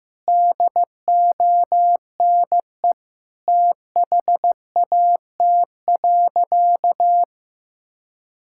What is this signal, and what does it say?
Morse code, heard as a keyed tone, about 15 wpm, 700 Hz. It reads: DONE THAT.
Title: Sentences from Top 1600 Words 15wpm